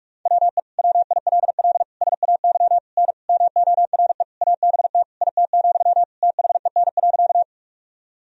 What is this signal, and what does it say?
Morse code, heard as a keyed tone, about 42 wpm, 700 Hz. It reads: WE WILL SAY N MORE ABT IT, THEN.
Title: Sentences from Top 200 Words and Abbreviations 45wpm